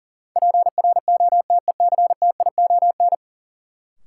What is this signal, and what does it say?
Morse code, heard as a keyed tone, about 39 wpm, 700 Hz. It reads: PROTECTION
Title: Sentences from Top 1600 Words - New Word Drill Alt - 40wpm